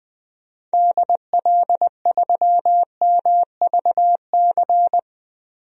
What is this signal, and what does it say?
Morse code, heard as a keyed tone, about 20 wpm, 700 Hz. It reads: DL3MVC
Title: DXCC-100 Call Signs - 6 Characters - Repeated 3x - 20wpm